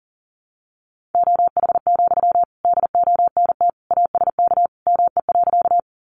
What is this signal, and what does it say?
Morse code, heard as a keyed tone, about 40 wpm, 700 Hz. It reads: OH, DONT ASK ME.
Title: Sentences from Top 500 Words 40wpm